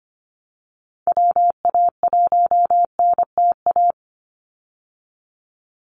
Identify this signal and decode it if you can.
Morse code, heard as a keyed tone, about 25 wpm, 700 Hz. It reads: WA1NTA
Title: Call Signs Repeated 2x - 25wpm